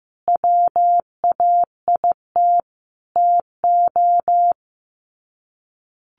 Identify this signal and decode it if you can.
Morse code, heard as a keyed tone, about 15 wpm, 700 Hz. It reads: WAIT TO
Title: Sets of 5 Words Encore 15wpm